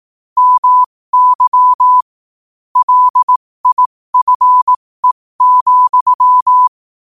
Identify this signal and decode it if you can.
Morse code, heard as a keyed tone, about 18 wpm, spaced as for 15 wpm, 1000 Hz. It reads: MY LIFE,
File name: qotdfiles / 15wpm